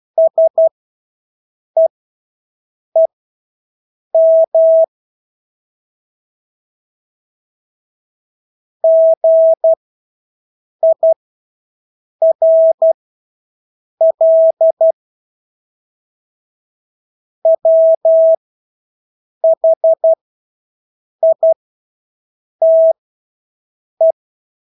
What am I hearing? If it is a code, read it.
Morse code, heard as a keyed tone, about 12 wpm, spaced as for 6 wpm, 650 Hz. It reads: SEEM GIRL WHITE
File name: Engelske ord 12-6 wpm